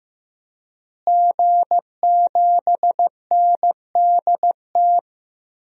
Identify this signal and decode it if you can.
Morse code, heard as a keyed tone, about 15 wpm, 700 Hz. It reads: G7NDT
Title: DXCC-100 Call Signs - 5 Characters - Repeated 2x - 15wpm